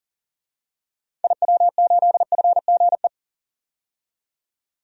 Morse code, heard as a keyed tone, about 40 wpm, 700 Hz. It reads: IW8FGE
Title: DXCC-100 Call Signs - 6 Characters - 40wpm